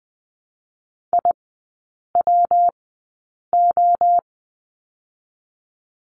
Morse code, heard as a keyed tone, about 20 wpm, 700 Hz. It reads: I W O
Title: 114 Lesson 19e - Letter w - Three Characters 20wpm